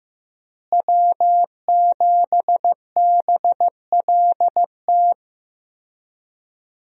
Morse code, heard as a keyed tone, about 14 wpm, 700 Hz. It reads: W7BLT
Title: USA Call Signs - 5 Characters - Repeated 2x - 15wpm